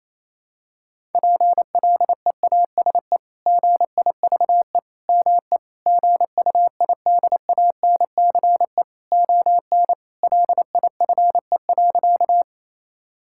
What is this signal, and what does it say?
Morse code, heard as a keyed tone, about 28 wpm, 700 Hz. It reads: PLEASE GIVE ME GUIDANCE ON LIFE.
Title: Sentences from Top 1600 Words 28wpm